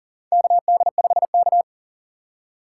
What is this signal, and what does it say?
Morse code, heard as a keyed tone, about 40 wpm, 700 Hz. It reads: KD5K
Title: USA Call Signs - 4 Characters - 3x Character Spacing - 40wpm